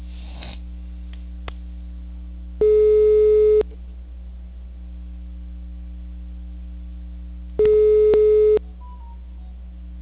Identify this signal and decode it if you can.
Ringback tone (Europe, 425 Hz).